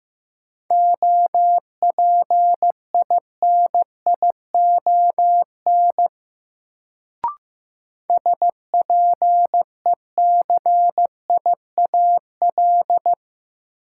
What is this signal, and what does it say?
Morse code, heard as a keyed tone, about 15 wpm, 700 Hz. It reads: OPINION SPECIAL
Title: Sentences from Top 1200 Words - New Word Drill Alt - 15wpm